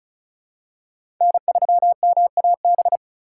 Morse code, heard as a keyed tone, about 36 wpm, 700 Hz. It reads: N3MAB
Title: USA Call Signs - 5 Characters - 2x Character Spacing - 35wpm